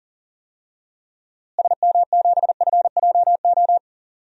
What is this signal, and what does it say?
Morse code, heard as a keyed tone, about 37 wpm, 700 Hz. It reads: SM7FJO
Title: DXCC-100 Call Signs - 6 Characters - Repeated 2x - 40wpm